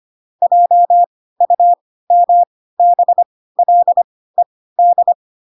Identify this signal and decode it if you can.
Morse code, heard as a keyed tone, about 25 wpm, spaced as for 16 wpm, 700 Hz. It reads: JUMBLED
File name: Pinocchio / Pinocchio ch7 10 25 ik1hge